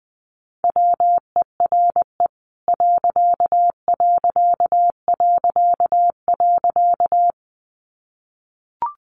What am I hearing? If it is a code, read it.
Morse code, heard as a keyed tone, about 20 wpm, 700 Hz. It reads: WERE ....
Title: A Sound Of Thunder 20wpm